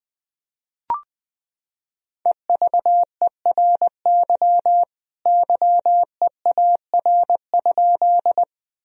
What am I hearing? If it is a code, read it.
Morse code, heard as a keyed tone, about 20 wpm, 700 Hz. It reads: EVERY YEAR?
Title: Sentences from Top 1000 Words - Review Part 1 - 20wpm